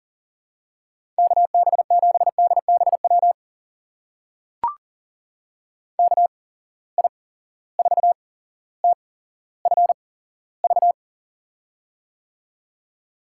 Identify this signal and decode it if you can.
Morse code, heard as a keyed tone, about 40 wpm, spaced as for 17 wpm, 700 Hz. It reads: KB7DBW KI4TFV
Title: USA Call Signs - 6 Characters - 8x Character Spacing - 40wpm